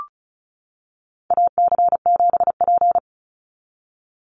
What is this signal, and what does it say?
Morse code, heard as a keyed tone, about 34 wpm, 700 Hz. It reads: AC7P
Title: NA SOTA Chaser Callsigns 35wpm